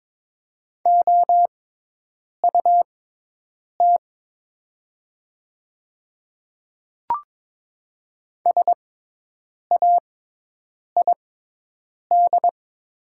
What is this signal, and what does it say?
Morse code, heard as a keyed tone, about 22 wpm, spaced as for 8 wpm, 700 Hz. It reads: OUT SAID
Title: 6x Character Spacing - Top 100 Words - 22wpm